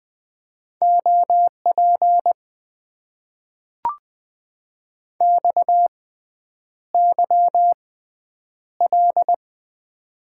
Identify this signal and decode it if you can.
Morse code, heard as a keyed tone, about 20 wpm, spaced as for 9 wpm, 700 Hz. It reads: OP XYL
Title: Top 90 QSO Elements - 6x Character Spacing - 20wpm